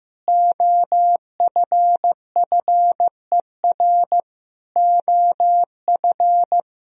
Morse code, heard as a keyed tone, about 15 wpm, 700 Hz. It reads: OFFER OF
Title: Sentences from Top 1000 Words - Review Part 4 - 15wpm